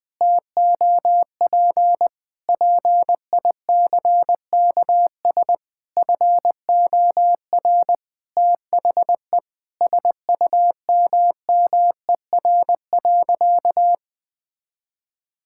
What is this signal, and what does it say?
Morse code, heard as a keyed tone, about 20 wpm, 700 Hz. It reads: TOP PICKS FOR THE SUMMER.
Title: Sentences with 9-Word Phrases - Spoken Beginning - 20wpm